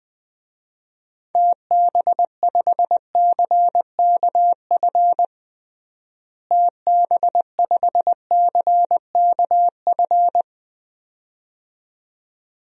Morse code, heard as a keyed tone, about 20 wpm, 700 Hz. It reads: TB5CKF TB5CKF
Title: DXCC-100 Call Signs - 6 Characters - Repeated 2x - 20wpm